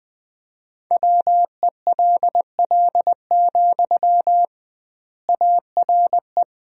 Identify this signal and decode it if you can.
Morse code, heard as a keyed tone, about 20 wpm, 700 Hz. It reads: WELL, ARE
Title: 2x Word-Spacing - Sentences from Top 400 Words - 20wpm